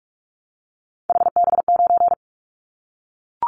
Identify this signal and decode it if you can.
Morse code, heard as a keyed tone, about 44 wpm, 700 Hz. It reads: HB9
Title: Callsign Prefixes for Top 25 DXCC - Repeated 3x - 45wpm